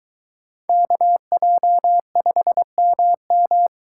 Morse code, heard as a keyed tone, about 23 wpm, 700 Hz. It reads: KJ5MM